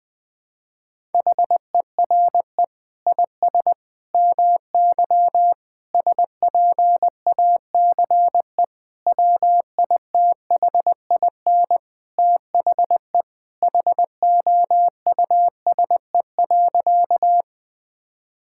Morse code, heard as a keyed tone, about 21 wpm, 700 Hz. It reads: HERE IS MY SPACE WITHIN THE HOUSE.
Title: Sentences from Top 1000 Words - Review Part 5 - 20wpm